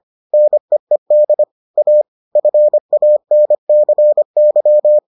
Morse code, heard as a keyed tone, about 25 wpm, 600 Hz. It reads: NEED A FANCY